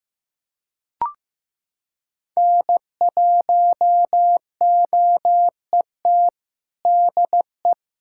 Morse code, heard as a keyed tone, about 15 wpm, 700 Hz. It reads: N1OET DE
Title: USA Callsign DE Callsign - 5 Characters - 15wpm